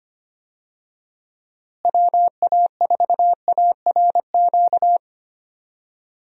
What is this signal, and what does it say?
Morse code, heard as a keyed tone, about 25 wpm, 700 Hz. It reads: WA4ARQ
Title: USA Call Signs - 6 Characters - Repeated 3x - 25wpm